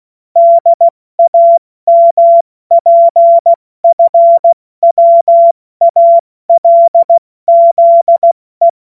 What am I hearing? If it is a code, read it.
Morse code, heard as a keyed tone, about 16 wpm, 680 Hz. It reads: DAMPFWALZE
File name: Morsekurs / Witze / Witz 02 April 2024 70 F80